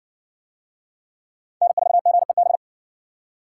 Morse code, heard as a keyed tone, about 55 wpm, 700 Hz. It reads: N4GEB